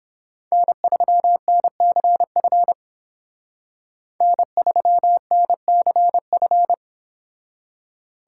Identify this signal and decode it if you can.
Morse code, heard as a keyed tone, about 30 wpm, 700 Hz. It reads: N3NCF N3NCF